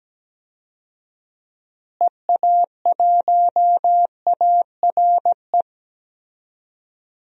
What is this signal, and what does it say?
Morse code, heard as a keyed tone, about 17 wpm, 700 Hz. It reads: EA1ARE
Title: DXCC-100 Call Signs - 6 Characters - Repeated 2x - 17wpm